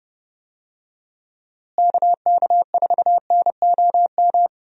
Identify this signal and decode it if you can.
Morse code, heard as a keyed tone, about 30 wpm, 700 Hz. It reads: KK4NOM
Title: USA Call Signs - 6 Characters - Repeated 3x - 30wpm